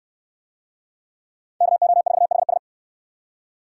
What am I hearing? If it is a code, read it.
Morse code, heard as a keyed tone, about 69 wpm, 700 Hz. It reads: KO4HS